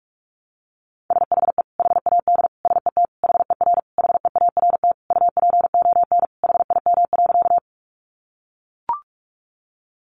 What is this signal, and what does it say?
Morse code, heard as a keyed tone, about 44 wpm, 700 Hz. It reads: SHE HAD SET HER HEART UPON HIM.
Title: Sentences from Top 1200 Words 45wpm